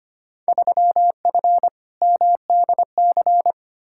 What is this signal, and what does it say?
Morse code, heard as a keyed tone, about 25 wpm, 700 Hz. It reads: 3F MDC
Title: Field Day Exchanges Encore - 25wpm